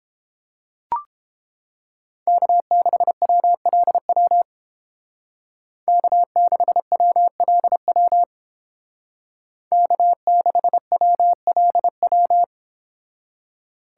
Morse code, heard as a keyed tone, about 29 wpm, 700 Hz. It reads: K6WLW K6WLW K6WLW